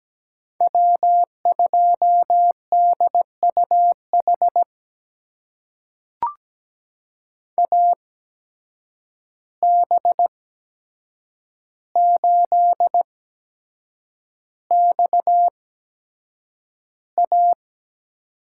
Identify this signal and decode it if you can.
Morse code, heard as a keyed tone, about 17 wpm, spaced as for 7 wpm, 700 Hz. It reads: W2DUH AB8XA
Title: USA Call Signs - 5 Characters - 8x Character Spacing - 17wpm